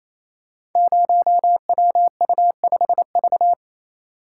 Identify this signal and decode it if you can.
Morse code, heard as a keyed tone, about 27 wpm, 700 Hz. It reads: 0WU5V
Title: Code Groups - 5 Letter:Numbers - 28wpm